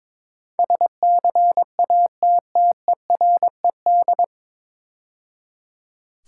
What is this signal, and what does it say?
Morse code, heard as a keyed tone, about 22 wpm, 700 Hz. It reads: SCATTERED
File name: 9 Letter Words - First 3 Letters Given - 22wpm